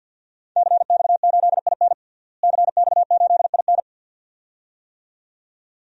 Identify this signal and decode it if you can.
Morse code, heard as a keyed tone, about 52 wpm, 700 Hz. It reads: CX8IN CX8IN
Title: DXCC-100 Call Signs - 5 Characters - Repeated 2x - 50wpm